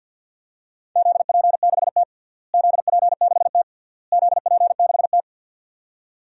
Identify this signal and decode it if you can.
Morse code, heard as a keyed tone, about 52 wpm, 700 Hz. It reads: ZP6T ZP6T ZP6T
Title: DXCC-100 Call Signs - 4 Characters - Repeated 3x - 50wpm